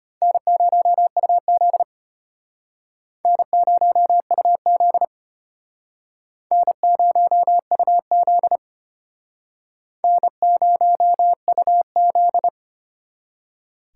Morse code, heard as a keyed tone, about 32 wpm, 700 Hz. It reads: N0UZ N0UZ N0UZ N0UZ